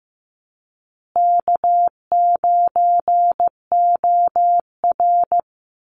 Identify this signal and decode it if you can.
Morse code, heard as a keyed tone, about 15 wpm, 700 Hz. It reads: K9OR
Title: Easy Call Signs Repeated 2x - 15wpm